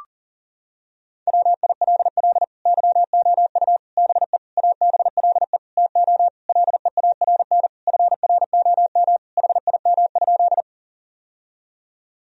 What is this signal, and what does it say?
Morse code, heard as a keyed tone, about 40 wpm, 700 Hz. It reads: WILL YOU BE ABLE TO LEARN FROM HIM?
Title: Sentences from Top 1000 Words - Review Part 2 - 40wpm